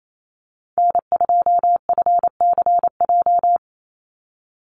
Morse code, heard as a keyed tone, about 28 wpm, 700 Hz. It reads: N2FCJ
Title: USA Call Signs 28wpm